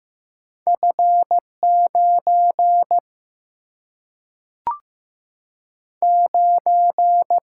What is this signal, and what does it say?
Morse code, heard as a keyed tone, about 16 wpm, 700 Hz. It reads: F 9 9
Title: Single-Letter Single-Number 15wpm